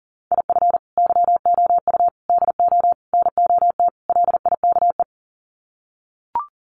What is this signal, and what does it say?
Morse code, heard as a keyed tone, about 40 wpm, 700 Hz. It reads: IF YOU DO NOT LIKE
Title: Sets of 5 Words 40wpm